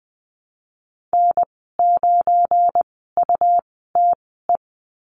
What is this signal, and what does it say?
Morse code, heard as a keyed tone, about 20 wpm, spaced as for 14 wpm, 700 Hz. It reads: N9UTE
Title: 173 Lesson 28g - Number 9 - Call Signs 20wpm